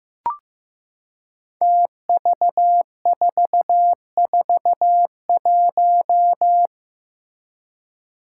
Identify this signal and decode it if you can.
Morse code, heard as a keyed tone, about 15 wpm, 700 Hz. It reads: TV441
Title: Five Challenging Letter-Number Combinations 15wpm